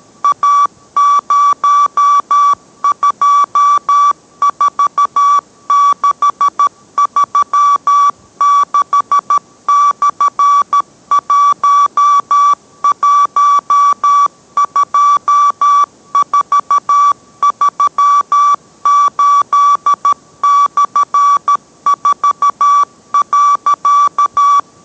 Morse code, heard as a keyed tone, about 13 wpm, 1150 Hz. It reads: A024636/112438/4.